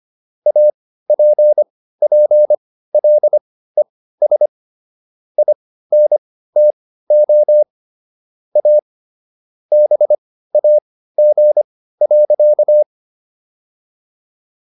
Morse code, heard as a keyed tone, about 25 wpm, spaced as for 15 wpm, 600 Hz. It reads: APPLES INTO A BAG.